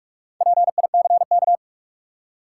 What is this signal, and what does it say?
Morse code, heard as a keyed tone, about 44 wpm, 700 Hz. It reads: PICK
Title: Four Letter Words - 45wpm